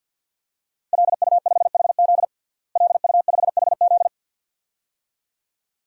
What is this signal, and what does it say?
Morse code, heard as a keyed tone, about 51 wpm, 700 Hz. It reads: LU5HZ LU5HZ
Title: DXCC-100 Call Signs - 5 Characters - Repeated 2x - 50wpm